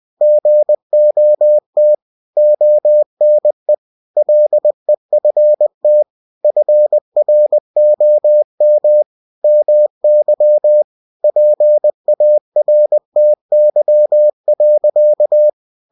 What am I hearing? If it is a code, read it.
Morse code, heard as a keyed tone, about 20 wpm, 600 Hz. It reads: GOT ONE LEFT FROM MY PARTY.
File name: cw-books / winnie-the-pooh / winnie-20wpm-page-0049